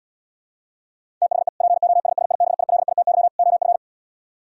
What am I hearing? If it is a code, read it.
Morse code, heard as a keyed tone, about 79 wpm, 700 Hz. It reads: THE POSSIBILITY OF